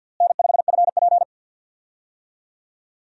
Morse code, heard as a keyed tone, about 50 wpm, 700 Hz. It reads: N5FP
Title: USA Call Signs - 4 Characters - Repeated 2x - 50wpm